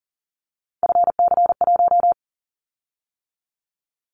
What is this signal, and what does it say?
Morse code, heard as a keyed tone, about 38 wpm, 700 Hz. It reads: FC1
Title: Callsign Prefixes for Top 50 DXCC - 40wpm